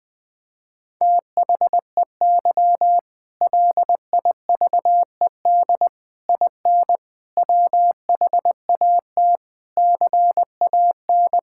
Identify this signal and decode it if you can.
Morse code, heard as a keyed tone, about 20 wpm, 700 Hz. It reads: THEY LIVED IN WHAT CAN
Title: Sentences with 8-Word Phrases - 20wpm